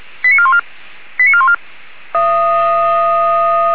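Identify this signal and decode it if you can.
Five-tone selective call (ZVEI), heard as quick runs of five tones, tones of about 70 ms. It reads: 89414 89414
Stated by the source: ZVEI five-tone call 89410 received by SDR.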